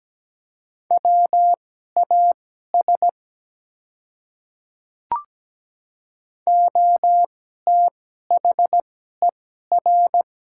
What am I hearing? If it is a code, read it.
Morse code, heard as a keyed tone, about 17 wpm, spaced as for 12 wpm, 700 Hz. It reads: WAS OTHER
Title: 2x Character Spacing - Top 100 Words - 17wpm